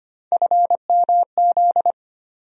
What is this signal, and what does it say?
Morse code, heard as a keyed tone, about 25 wpm, 700 Hz. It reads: FMZ
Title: USA Call Sign Postfixes - 3 Letters 25wpm